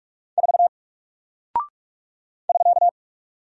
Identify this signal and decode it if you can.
Morse code, heard as a keyed tone, about 44 wpm, 700 Hz. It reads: 4 <SK>
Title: Single Character (Letter:Number:Punctuation:Prosign) 45wpm